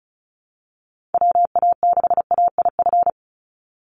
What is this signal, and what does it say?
Morse code, heard as a keyed tone, about 34 wpm, 700 Hz. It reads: WA6AIF
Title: USA Call Signs 35wpm